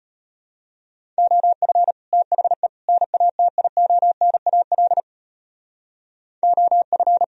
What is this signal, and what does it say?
Morse code, heard as a keyed tone, about 32 wpm, 700 Hz. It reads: OF THE NATIONAL OF